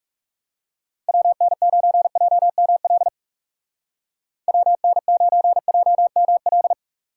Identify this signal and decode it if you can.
Morse code, heard as a keyed tone, about 37 wpm, 700 Hz. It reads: WN9JML WN9JML
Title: Speed Racing - USA 2x3 Call Signs 30wpm